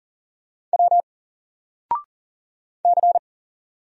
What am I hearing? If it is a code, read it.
Morse code, heard as a keyed tone, about 40 wpm, 700 Hz. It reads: W C
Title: Single Character (Letter:Number:Punctuation:Prosign) Alt 40wpm